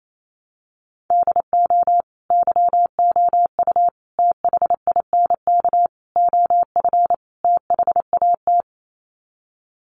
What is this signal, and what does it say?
Morse code, heard as a keyed tone, about 28 wpm, 700 Hz. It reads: DO YOU THINK OF THAT
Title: Sets of 5 Words Encore2 28wpm